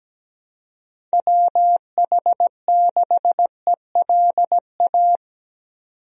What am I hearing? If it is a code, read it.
Morse code, heard as a keyed tone, about 17 wpm, 700 Hz. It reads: WH6ELA
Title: USA Call Signs - 6 Characters - 17wpm